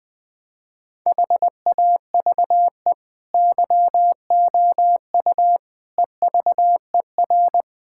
Morse code, heard as a keyed tone, about 20 wpm, 700 Hz. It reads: HAVE YOU EVER
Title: Difficult Sentences with Hints - 20wpm